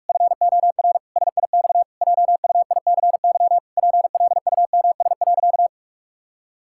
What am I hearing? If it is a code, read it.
Morse code, heard as a keyed tone, about 41 wpm, 700 Hz. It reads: FOR SIX JUICY PLUMS.
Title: Pangram Challenge v2 - 45wpm